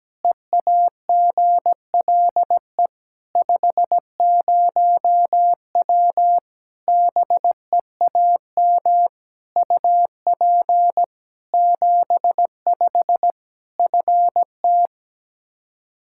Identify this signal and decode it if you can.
Morse code, heard as a keyed tone, about 17 wpm, 700 Hz. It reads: EAGLE 50W BEAM UP 75 FT
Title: Practice QSO - Exchange 2 - Info - 17wpm